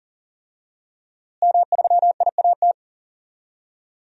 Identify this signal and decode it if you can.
Morse code, heard as a keyed tone, about 40 wpm, 700 Hz. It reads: M3IAT
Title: DXCC-100 Call Signs - 5 Characters - Repeated 2x - 40wpm